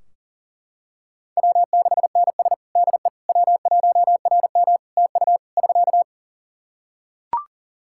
Morse code, heard as a keyed tone, about 37 wpm, 700 Hz. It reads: W6NS DE W1RM TU <SK>
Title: K1USN SST QSO Practice 40wpm